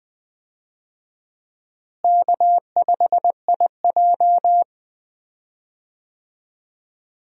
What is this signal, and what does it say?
Morse code, heard as a keyed tone, about 20 wpm, 700 Hz. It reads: K5IJ
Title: USA Call Signs - 4 Characters - Repeated 2x - 20wpm